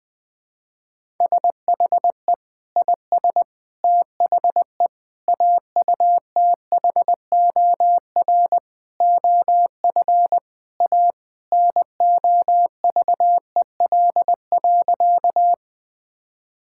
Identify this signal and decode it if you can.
Morse code, heard as a keyed tone, about 20 wpm, 700 Hz. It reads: SHE IS THE AUTHOR OF A NOVEL.
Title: Sentences with 4-Word Phrases - Spoken Beginning - 20wpm